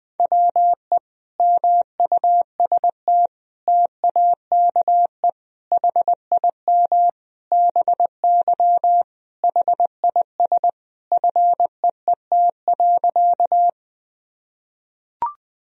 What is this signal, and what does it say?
Morse code, heard as a keyed tone, about 20 wpm, 700 Hz. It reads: WE MUST TAKE HIM BY HIS FEET.
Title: Sentences from Top 1000 Words - Review Part 1 - 20wpm